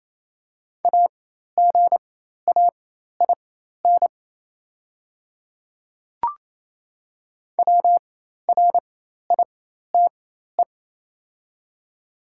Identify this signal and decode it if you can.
Morse code, heard as a keyed tone, about 28 wpm, spaced as for 13 wpm, 700 Hz. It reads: AGAIN WRITE
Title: 4x Character Spacing - Top 500 Words - 28wpm